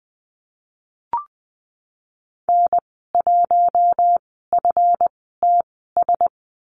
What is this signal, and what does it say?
Morse code, heard as a keyed tone, about 20 wpm, spaced as for 15 wpm, 700 Hz. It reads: N1FTS
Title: 179 Lesson 29g - Slash - Call Signs 20wpm